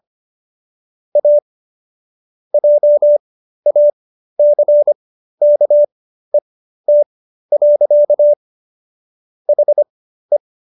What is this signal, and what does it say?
Morse code, heard as a keyed tone, about 25 wpm, spaced as for 13 wpm, 590 Hz. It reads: A JACKET. HE